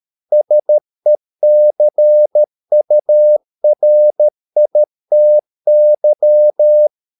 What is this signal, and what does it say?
Morse code, heard as a keyed tone, about 13 wpm, 600 Hz. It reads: SECURITY